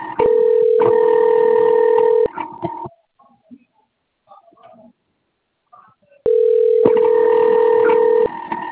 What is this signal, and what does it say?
Ringback tone (North America), 440 + 480 Hz.